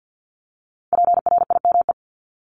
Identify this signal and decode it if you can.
Morse code, heard as a keyed tone, about 50 wpm, 700 Hz. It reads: PRIME
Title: Multi-letter Phonemes Repeated 3x 50wpm